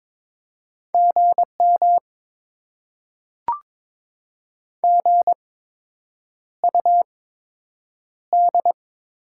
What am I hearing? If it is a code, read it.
Morse code, heard as a keyed tone, about 22 wpm, spaced as for 8 wpm, 700 Hz. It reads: GM GUD
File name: Top 20 QSO Elements - 8x Character Spacing - 22wpm